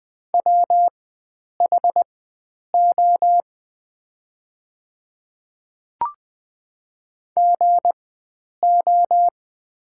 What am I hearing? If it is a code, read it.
Morse code, heard as a keyed tone, about 20 wpm, spaced as for 9 wpm, 690 Hz. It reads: WHO GO
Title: 4x Character Spacing - Top 100 Words - 20wpm